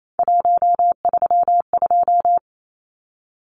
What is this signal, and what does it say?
Morse code, heard as a keyed tone, about 28 wpm, 700 Hz. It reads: 132
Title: Numbers - 3-Digit 28wpm